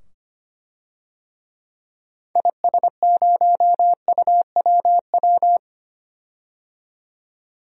Morse code, heard as a keyed tone, about 25 wpm, 700 Hz. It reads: IS0UWW
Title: DXCC-100 Call Signs - 6 Characters - 25wpm